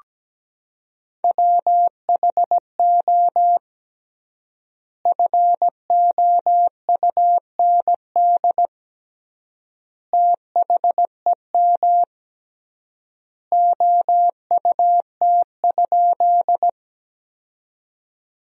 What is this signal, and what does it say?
Morse code, heard as a keyed tone, about 17 wpm, 700 Hz. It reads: WHO FOUND THEM OUT?
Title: 3x Word-Spacing - Sentences from Top 300 Words - 17wpm